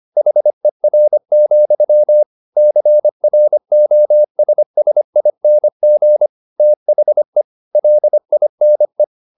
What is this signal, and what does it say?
Morse code, heard as a keyed tone, about 25 wpm, 600 Hz. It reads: HER, CROSSING THE LINE
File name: the old man and the sea part 1 25wpm